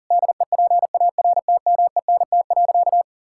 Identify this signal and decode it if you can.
Morse code, heard as a keyed tone, about 40 wpm, 700 Hz. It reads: DEPARTMENT.
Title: Sentences from Top 1000 Words - Review Part 5 - 40wpm